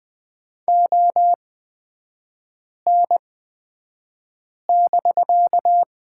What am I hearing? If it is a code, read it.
Morse code, heard as a keyed tone, about 20 wpm, spaced as for 5 wpm, 700 Hz. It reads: ON<BK>
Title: 250 Lesson 40e - BK - Three Characters 20wpm